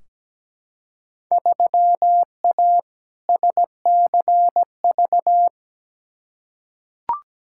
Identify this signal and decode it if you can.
Morse code, heard as a keyed tone, about 17 wpm, 700 Hz. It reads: 3A SCV
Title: Field Day Exchanges - 17wpm